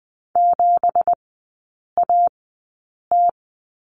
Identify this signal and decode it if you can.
Morse code, heard as a keyed tone, about 20 wpm, spaced as for 8 wpm, 690 Hz. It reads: 7AT
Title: 166 Lesson 27e - Number 7 - Three Characters 20wpm